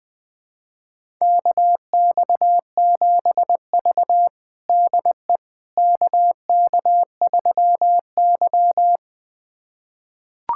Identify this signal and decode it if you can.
Morse code, heard as a keyed tone, about 20 wpm, 700 Hz. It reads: KX7V DE KK3Y
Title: USA Callsign DE Callsign - 4 Characters - 20wpm